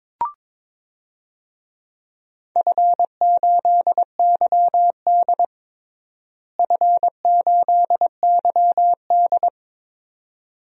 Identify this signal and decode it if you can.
Morse code, heard as a keyed tone, about 22 wpm, 700 Hz. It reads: F8YD F8YD
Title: DXCC-100 Call Signs - 4 Characters - Repeated 2x - 22wpm